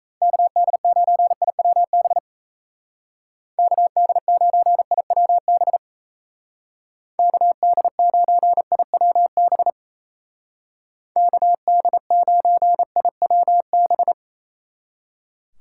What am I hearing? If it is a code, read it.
Morse code, heard as a keyed tone, about 35 wpm, 700 Hz. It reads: KD9IWB KD9IWB KD9IWB KD9IWB
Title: Speed Racing - USA Call Signs 28wpm